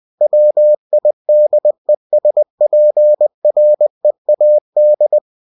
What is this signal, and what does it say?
Morse code, heard as a keyed tone, about 20 wpm, 600 Hz. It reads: WIDESPREAD